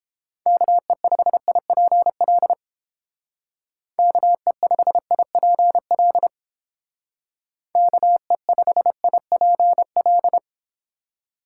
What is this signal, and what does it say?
Morse code, heard as a keyed tone, about 30 wpm, 700 Hz. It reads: KE5IPL KE5IPL KE5IPL